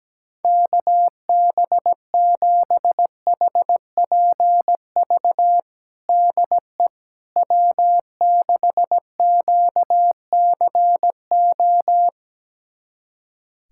Morse code, heard as a keyed tone, about 17 wpm, 700 Hz. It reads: KB7HPV DE W6QCO
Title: USA Callsign DE Callsign 17wpm